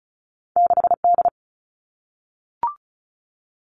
Morse code, heard as a keyed tone, about 36 wpm, 700 Hz. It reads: 6D
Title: All Cut Numbers - 2-Digit 35wpm